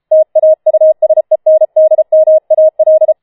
Morse code, heard as a keyed tone, about 33 wpm, 630 Hz. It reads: TAUSENDMAL
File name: Schneewittchen Teil5 34wpm